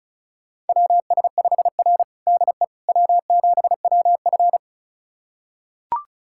Morse code, heard as a keyed tone, about 36 wpm, 700 Hz. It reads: WS5R DE W7WF
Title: USA Callsign DE Callsign - 4 Characters - 35wpm